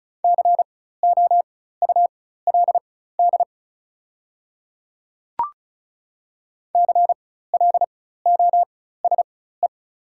Morse code, heard as a keyed tone, about 36 wpm, spaced as for 16 wpm, 700 Hz. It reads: COULD CLOSE
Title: 4x Character Spacing - Top 300 Words - 35wpm